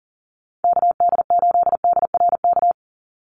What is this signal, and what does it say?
Morse code, heard as a keyed tone, about 40 wpm, 690 Hz. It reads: KD8DRK
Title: USA Call Signs 40wpm